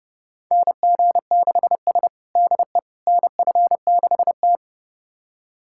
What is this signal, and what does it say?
Morse code, heard as a keyed tone, about 30 wpm, 700 Hz. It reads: NG6S DE NF6T
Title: USA Callsign DE Callsign - 4 Characters - 30wpm